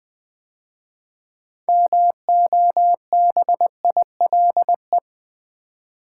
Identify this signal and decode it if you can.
Morse code, heard as a keyed tone, about 20 wpm, 700 Hz. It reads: MOBILE
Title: 6 Letter Words - First 3 Letters Given - 20wpm